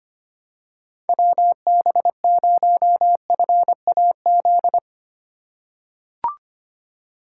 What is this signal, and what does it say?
Morse code, heard as a keyed tone, about 25 wpm, 700 Hz. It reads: WB0FAZ
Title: USA Call Signs - 6 Characters - 3x Character Spacing - 25wpm